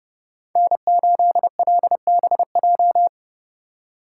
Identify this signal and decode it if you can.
Morse code, heard as a keyed tone, about 37 wpm, 700 Hz. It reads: N8LBJ
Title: Speed Racing - USA 1x3 Call Signs 30wpm